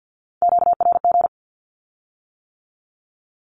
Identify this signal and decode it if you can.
Morse code, heard as a keyed tone, about 50 wpm, 700 Hz. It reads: QRG
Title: Q-Codes with Spoken Meaning - 50wpm